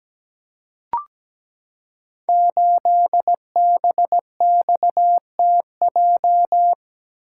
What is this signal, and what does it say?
Morse code, heard as a keyed tone, about 17 wpm, 700 Hz. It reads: 8BXTJ
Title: Code Groups - 5 Letter:Numbers - 17wpm